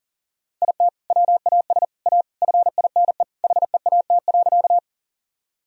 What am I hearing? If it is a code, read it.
Morse code, heard as a keyed tone, about 40 wpm, 700 Hz. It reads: IT WAS A FINE HEAT.
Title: Sentences from Top 1000 Words - Review Part 3 - 40wpm